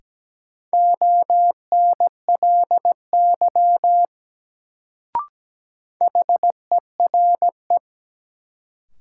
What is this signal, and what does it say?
Morse code, heard as a keyed tone, about 17 wpm, 700 Hz. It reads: ONLY HERE
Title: Sentences from Top 200 Words - New Word Drill - 17wpm